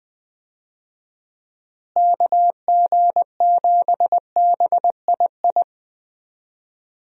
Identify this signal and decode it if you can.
Morse code, heard as a keyed tone, about 20 wpm, 700 Hz. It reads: KG7BII
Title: USA Call Signs - 6 Characters - Repeated 2x - 20wpm